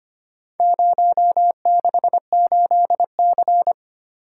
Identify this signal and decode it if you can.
Morse code, heard as a keyed tone, about 25 wpm, 700 Hz. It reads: 068C
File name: practice / 25wpm / Four Letter-Number Combinations - 25wpm